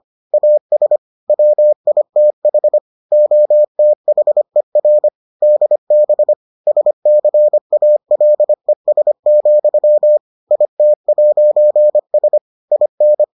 Morse code, heard as a keyed tone, about 25 wpm, 600 Hz. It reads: AS WITH OTHER DB SCALES, IT'S IN